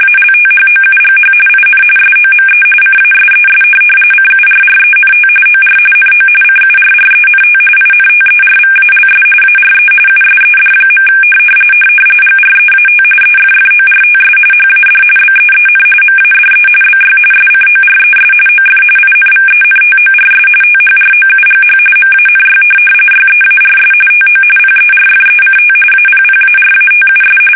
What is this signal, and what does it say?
RTTY, about 45.45 baud. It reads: ALTHOUGH IN SOME CASES CURRENT OR NO CURRENT INTERVALS ARE USED. GREAT EFFORTS HAD BEEN MADE BY SCORES OF INVENTORS THROUGH MANY YEARS TO PERFECT APPARATUS
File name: fsk850